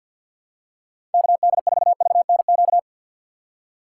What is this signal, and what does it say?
Morse code, heard as a keyed tone, about 53 wpm, 700 Hz. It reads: KD4VNQ
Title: USA Call Signs - 6 Characters - 4x Character Spacing - 50wpm